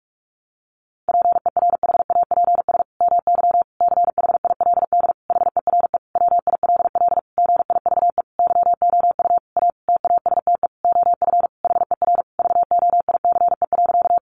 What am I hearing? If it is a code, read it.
Morse code, heard as a keyed tone, about 44 wpm, 700 Hz. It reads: PERHAPS MY CHILD HERE WILL GIVE YOU A TASTE OF HER VOICE.
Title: Sentences from Top 1100 Words 45wpm